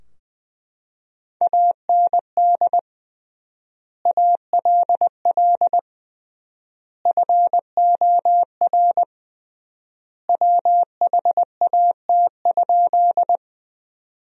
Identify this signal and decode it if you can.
Morse code, heard as a keyed tone, about 21 wpm, 700 Hz. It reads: AND ALL FOR WHAT?
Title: 3x Word-Spacing - Sentences from Top 100 Words - 20wpm